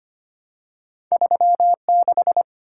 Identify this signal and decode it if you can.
Morse code, heard as a keyed tone, about 25 wpm, 700 Hz. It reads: 36
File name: Two Letter-Number Combinations - 25wpm